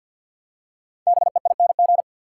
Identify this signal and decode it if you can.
Morse code, heard as a keyed tone, about 51 wpm, 700 Hz. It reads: BEING
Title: Five Letter Words - 50wpm